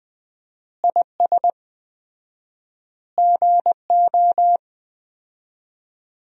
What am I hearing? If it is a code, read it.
Morse code, heard as a keyed tone, about 20 wpm, 700 Hz. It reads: IS GO
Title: 4x Word-Spacing - Sentences from Top 200 Words - 20wpm